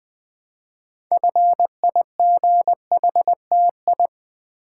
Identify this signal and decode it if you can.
Morse code, heard as a keyed tone, about 20 wpm, 700 Hz. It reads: FIGHTI
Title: Build A Word - 2 Letters At A Time - 8-Letter Words - 20wpm